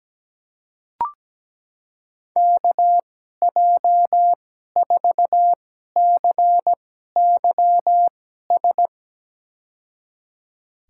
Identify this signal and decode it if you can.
Morse code, heard as a keyed tone, about 17 wpm, spaced as for 12 wpm, 700 Hz. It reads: KJ4CYS
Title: USA Call Signs - 2x Character Spacing - 17wpm